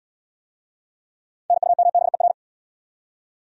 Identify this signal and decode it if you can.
Morse code, heard as a keyed tone, about 63 wpm, 700 Hz. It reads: NUMBER